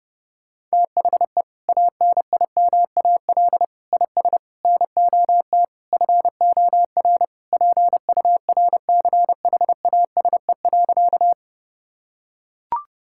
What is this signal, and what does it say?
Morse code, heard as a keyed tone, about 32 wpm, 700 Hz. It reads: THE ANIMAL IS NOT FOR PURCHASE.
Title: Sentences from Top 1000 Words - Review Part 5 - 30wpm